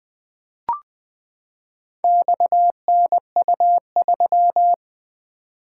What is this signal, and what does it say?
Morse code, heard as a keyed tone, about 20 wpm, 700 Hz. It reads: XNU3
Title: Four Letter-Number Combinations - 20wpm